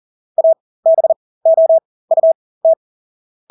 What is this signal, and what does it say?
Morse code, heard as a keyed tone, about 40 wpm, spaced as for 20 wpm, 650 Hz. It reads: ABOUT
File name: Engelske ord 40-20 wpm